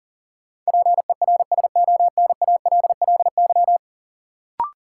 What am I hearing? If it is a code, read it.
Morse code, heard as a keyed tone, about 40 wpm, 700 Hz. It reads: PERSONALLY
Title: Sentences from Top 1400 Words - New Word Drill - 40wpm